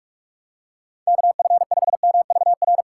Morse code, heard as a keyed tone, about 42 wpm, 700 Hz. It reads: KF5MVR
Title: USA Call Signs - 3x Character Spacing - 45wpm